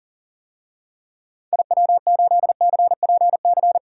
Morse code, heard as a keyed tone, about 40 wpm, 700 Hz. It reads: IW8CPC
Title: DXCC-100 Call Signs - 6 Characters - 40wpm